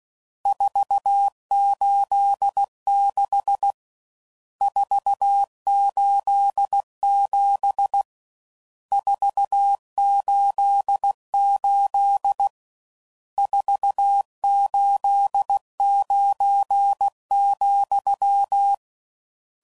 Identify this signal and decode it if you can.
Morse code, heard as a keyed tone, about 16 wpm, 800 Hz. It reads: 486 487 488 489,
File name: CW Code Groups @15WPM / RSTs / Various RST Nos